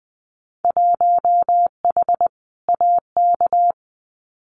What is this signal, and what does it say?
Morse code, heard as a keyed tone, about 20 wpm, 700 Hz. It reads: 1H AK
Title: Winter Field Day Exchanges Encore 20wpm